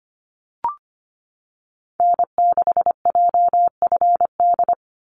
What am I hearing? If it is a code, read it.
Morse code, heard as a keyed tone, about 25 wpm, 690 Hz. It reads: N6JFD
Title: NA SOTA Activator Callsigns 25wpm